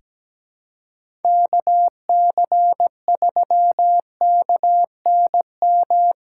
Morse code, heard as a keyed tone, about 17 wpm, 700 Hz. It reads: KC3KNM
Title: USA Callsign DE Callsign - 6 Characters - 17wpm